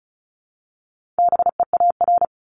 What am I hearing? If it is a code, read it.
Morse code, heard as a keyed tone, about 36 wpm, 700 Hz. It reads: BEAR
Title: Multi-letter Phonemes 35wpm